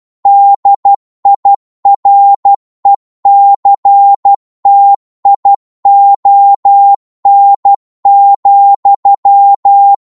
Morse code, heard as a keyed tone, about 12 wpm, 800 Hz. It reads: DIRECTION,